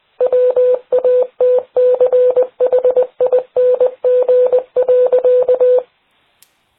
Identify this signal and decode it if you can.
Morse code, heard as a keyed tone, about 20 wpm, 500 Hz. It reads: WATCHING.